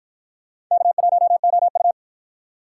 Morse code, heard as a keyed tone, about 49 wpm, 700 Hz. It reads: K1OU